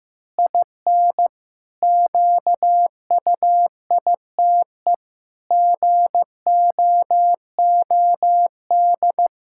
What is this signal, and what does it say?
Morse code, heard as a keyed tone, about 15 wpm, 700 Hz. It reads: IN QUITE GOOD
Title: Pangram Challenge v2 - 15wpm